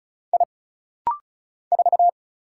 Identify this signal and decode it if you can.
Morse code, heard as a keyed tone, about 35 wpm, 700 Hz. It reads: I 4
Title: Single Letter-Number - Rapid-Fire 35wpm